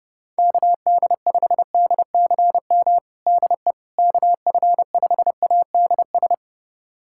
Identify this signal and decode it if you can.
Morse code, heard as a keyed tone, about 30 wpm, 700 Hz. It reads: KD5DCM DE KF5ADS
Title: USA Callsign DE Callsign 30wpm